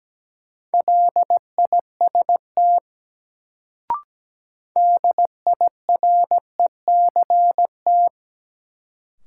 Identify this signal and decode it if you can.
Morse code, heard as a keyed tone, about 17 wpm, 700 Hz. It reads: LIST DIRECT
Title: Sentences from Top 400 Words - New Word Drill - 17wpm